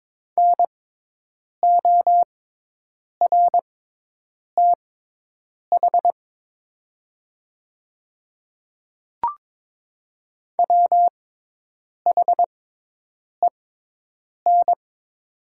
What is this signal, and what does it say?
Morse code, heard as a keyed tone, about 22 wpm, spaced as for 8 wpm, 700 Hz. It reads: NORTH WHEN
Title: 6x Character Spacing - Top 300 Words - 22wpm